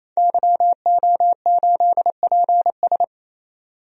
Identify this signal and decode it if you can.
Morse code, heard as a keyed tone, about 28 wpm, 700 Hz. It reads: YO8PS
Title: DXCC-100 Call Signs - 5 Characters - 28wpm